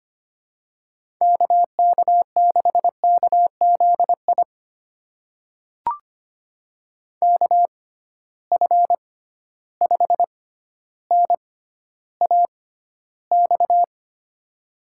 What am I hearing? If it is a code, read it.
Morse code, heard as a keyed tone, about 25 wpm, spaced as for 13 wpm, 700 Hz. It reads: KK6KZI KF5NAX
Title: USA Call Signs - 6 Characters - 6x Character Spacing - 25wpm